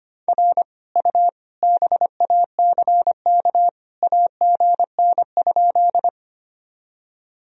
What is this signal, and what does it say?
Morse code, heard as a keyed tone, about 25 wpm, 700 Hz. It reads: R U BACK AGN?
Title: Sentences from Top 200 Words and Abbreviations 25wpm